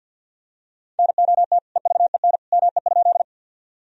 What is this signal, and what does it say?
Morse code, heard as a keyed tone, about 52 wpm, 700 Hz. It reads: NOT EVEN ME?
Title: Sentences from Top 1000 Words - Review Part 1 - 50wpm